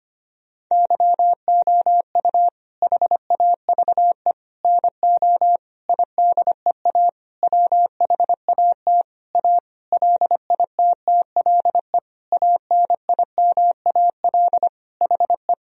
Morse code, heard as a keyed tone, about 25 wpm, 700 Hz. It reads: YOU HAVE NO IDEA WHAT A LITTLE ANIMAL HE
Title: Sentences from Top 1000 Words - Review Part 1 - 25wpm